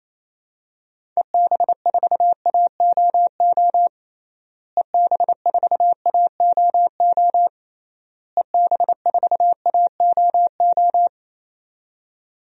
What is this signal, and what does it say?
Morse code, heard as a keyed tone, about 28 wpm, 700 Hz. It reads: EB4AOO EB4AOO EB4AOO
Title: DXCC-100 Call Signs - 6 Characters - Repeated 3x - 28wpm